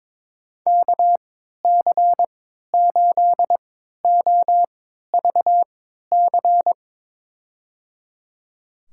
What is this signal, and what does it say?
Morse code, heard as a keyed tone, about 22 wpm, spaced as for 12 wpm, 700 Hz. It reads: KC8OVC
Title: USA Call Signs - 6 Characters - 3x Character Spacing - 22wpm